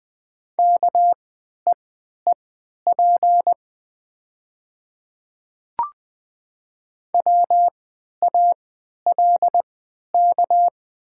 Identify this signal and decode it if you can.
Morse code, heard as a keyed tone, about 20 wpm, spaced as for 11 wpm, 700 Hz. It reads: KEEP WALK
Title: 3x Character Spacing - Top 500 Words - 20wpm